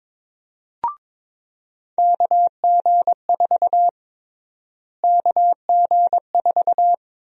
Morse code, heard as a keyed tone, about 22 wpm, 700 Hz. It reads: KG4 KG4
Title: USA Call Sign Prefixes - Repeated 2x - 22wpm